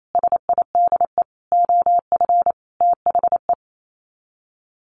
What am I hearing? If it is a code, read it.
Morse code, heard as a keyed tone, about 28 wpm, 700 Hz. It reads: SIDE OF THE
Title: Sets of 5 Words Encore 28wpm